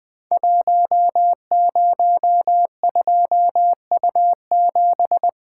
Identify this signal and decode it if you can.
Morse code, heard as a keyed tone, about 20 wpm, 700 Hz. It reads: 102U7
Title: Code Groups - 5 Letter:Numbers - 20wpm